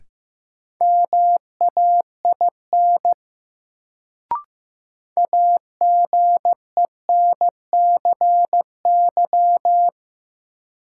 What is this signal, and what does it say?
Morse code, heard as a keyed tone, about 15 wpm, 700 Hz. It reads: MAIN AGENCY
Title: Sentences from Top 1300 Words - New Word Drill Alt - 15wpm